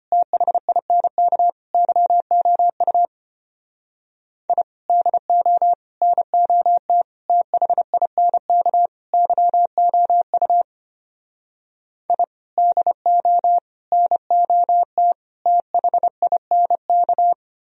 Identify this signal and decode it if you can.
Morse code, heard as a keyed tone, about 32 wpm, 700 Hz. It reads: THINK YOU I DO NOT THINK YOU I DO NOT THINK